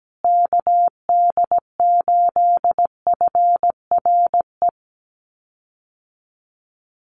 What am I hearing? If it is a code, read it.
Morse code, heard as a keyed tone, about 17 wpm, 700 Hz. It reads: KD8FRE
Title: USA Call Signs Repeated 2x 17wpm